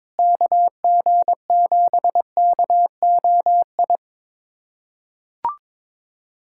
Morse code, heard as a keyed tone, about 22 wpm, 700 Hz. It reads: KG7KOI
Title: USA Callsign DE Callsign - 6 Characters - 22wpm